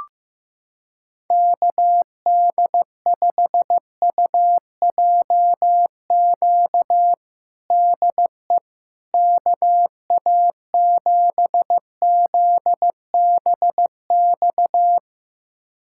Morse code, heard as a keyed tone, about 16 wpm, 700 Hz. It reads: KD5UJQ DE KA7ZBX
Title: USA Callsign DE Callsign - 6 Characters - 15wpm